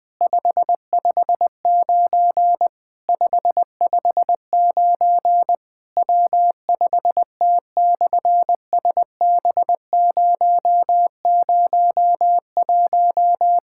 Morse code, heard as a keyed tone, about 20 wpm, 700 Hz. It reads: 559 559 W5T/SB001
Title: SOTA QSO Practice 20wpm